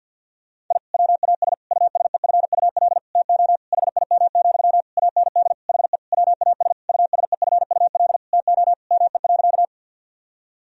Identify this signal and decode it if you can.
Morse code, heard as a keyed tone, about 50 wpm, 700 Hz. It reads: I WAS USEFUL TO HIM, AND HE WAS USEFUL TO ME.
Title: Sentences from Top 1000 Words - Review Part 4 - 50wpm